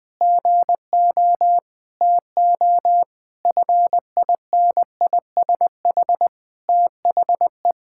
Morse code, heard as a keyed tone, about 20 wpm, 700 Hz. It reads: GO TO FINISH THE
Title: Sentences with 8-Word Phrases - 20wpm